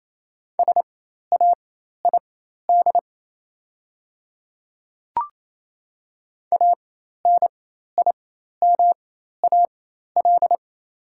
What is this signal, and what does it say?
Morse code, heard as a keyed tone, about 28 wpm, spaced as for 13 wpm, 700 Hz. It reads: SAID ANIMAL
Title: 4x Character Spacing - Top 200 Words - 28wpm